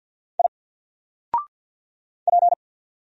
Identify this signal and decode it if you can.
Morse code, heard as a keyed tone, about 50 wpm, 700 Hz. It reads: I P
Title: Single Letter-Number 50wpm